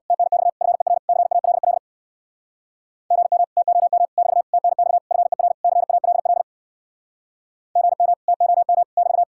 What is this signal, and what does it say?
Morse code, heard as a keyed tone, about 62 wpm, 700 Hz. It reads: TNX FER CALL GM TOM <BT> TNX FER CALL GM TOM <BT>
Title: Speed-Racing Common QSO Phrases - 50wpm